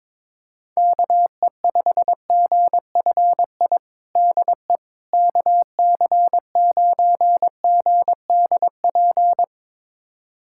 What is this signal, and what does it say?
Morse code, heard as a keyed tone, about 22 wpm, 700 Hz. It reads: KE5GFI DE KC9GDP
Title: USA Callsign DE Callsign 22wpm